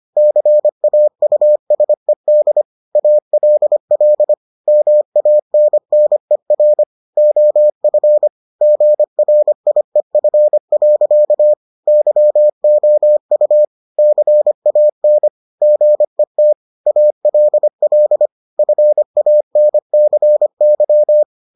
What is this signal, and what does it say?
Morse code, heard as a keyed tone, about 25 wpm, 600 Hz. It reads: CAUSED ALL MANNER OF GRIEF. YOU CAN GET ALL FANCY